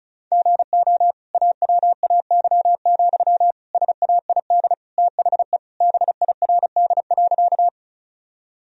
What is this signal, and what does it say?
Morse code, heard as a keyed tone, about 34 wpm, 700 Hz. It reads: GO AWAY, SAID THE BIRD.
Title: Sentences from Top 1000 Words - Review Part 2 - 35wpm